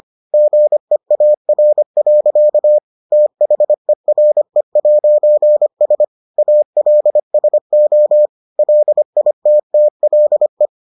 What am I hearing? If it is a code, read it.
Morse code, heard as a keyed tone, about 25 wpm, 600 Hz. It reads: GEAR. THERE'S ALSO LITTLE